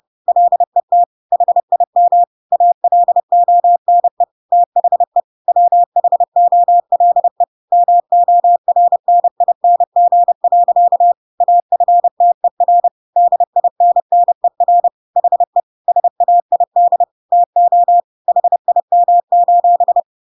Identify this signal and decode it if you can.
Morse code, heard as a keyed tone, about 30 wpm, 700 Hz. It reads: LET HIM ALONE THE WHOLE MORNING. AFTER DINNER HE SAID TO HIM:
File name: Pinocchio / Pinocchio ch8 30 30 ik1hge